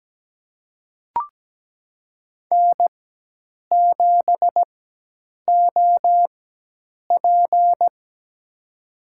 Morse code, heard as a keyed tone, about 17 wpm, spaced as for 8 wpm, 700 Hz. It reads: N7OP
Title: USA Call Signs - 4x Character Spacing - 17wpm